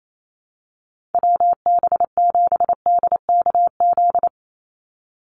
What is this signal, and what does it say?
Morse code, heard as a keyed tone, about 28 wpm, 700 Hz. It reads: WB7DKZ
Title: Call Signs - 28wpm